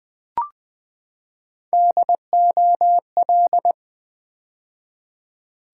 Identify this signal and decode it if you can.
Morse code, heard as a keyed tone, about 20 wpm, 700 Hz. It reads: DOL
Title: Build A Word - 3 Letters At A Time - 6-Letter Words - 20wpm